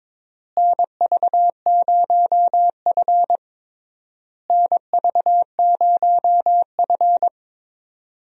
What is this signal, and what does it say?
Morse code, heard as a keyed tone, about 22 wpm, 700 Hz. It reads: NV0F NV0F
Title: USA Call Signs - 4 Characters - Repeated 3x - 22wpm